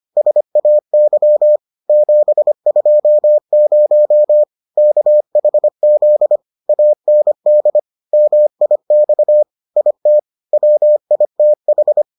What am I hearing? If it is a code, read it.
Morse code, heard as a keyed tone, about 25 wpm, 600 Hz. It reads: SAY 720 KHZ AND MIX IT WITH